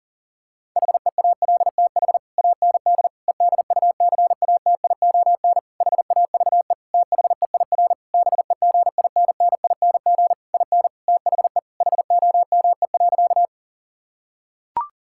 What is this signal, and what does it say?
Morse code, heard as a keyed tone, about 40 wpm, 700 Hz. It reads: HEALTH AND EDUCATION HAVE THEIR BEGINNING IN THE HOME.
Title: Sentences from Top 1000 Words - Review Part 6 - 40wpm